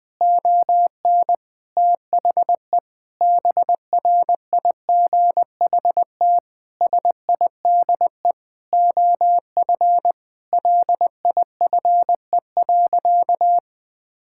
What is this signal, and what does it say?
Morse code, heard as a keyed tone, about 20 wpm, 700 Hz. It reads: ON THE BRIGHT SIDE OF LIFE.
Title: Sentences with 8-Word Phrases - 20wpm